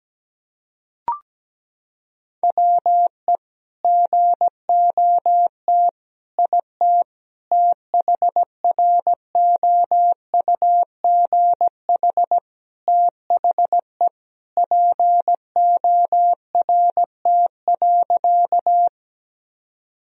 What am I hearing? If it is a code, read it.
Morse code, heard as a keyed tone, about 17 wpm, 700 Hz. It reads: WE GOT IT THROUGH THE PORT.
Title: Sentences from Top 1000 Words - Review Part 1 - 17wpm